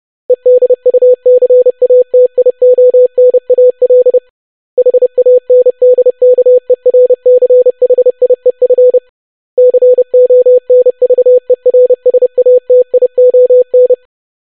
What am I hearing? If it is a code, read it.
Morse code, heard as a keyed tone, about 30 wpm, 500 Hz. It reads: EDUCATIONAL HANDKERCHIEF CONVERSATION
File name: longwords-25wpm